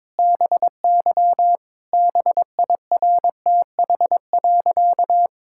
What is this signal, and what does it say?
Morse code, heard as a keyed tone, about 22 wpm, 700 Hz. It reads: BY BIRTH.
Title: Sentences from Top 1000 Words - Review Part 5 - 22wpm